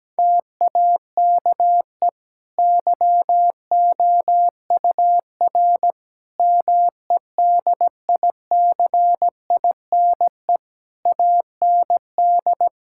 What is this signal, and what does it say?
Morse code, heard as a keyed tone, about 17 wpm, 700 Hz. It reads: TAKE YOUR MEDICINE AND
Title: Sentences from Top 1000 Words - Review Part 6 - 17wpm